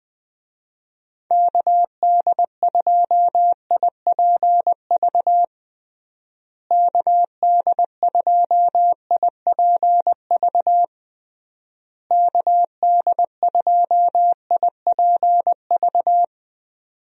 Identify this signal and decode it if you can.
Morse code, heard as a keyed tone, about 20 wpm, 700 Hz. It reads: KD2IPV KD2IPV KD2IPV
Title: USA Call Signs - 6 Characters - Repeated 3x - 20wpm